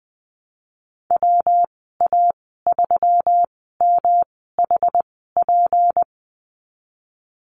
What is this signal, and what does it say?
Morse code, heard as a keyed tone, about 20 wpm, spaced as for 15 wpm, 700 Hz. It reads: WA3MHP
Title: 179 Lesson 29g - Slash - Call Signs 20wpm